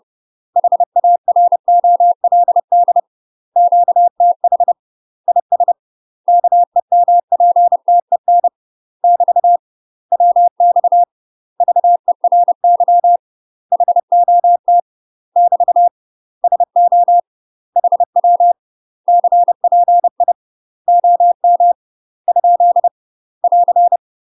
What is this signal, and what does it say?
Morse code, heard as a keyed tone, about 30 wpm, 700 Hz. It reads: HAROLD QTH IS KEMPTEN <BT> WX VERY HOT <BT> SO HW CPI OM ? <AR>
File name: UK MASTER QSOs TRAINING / 30WPM / UK MASTER QSOs 0013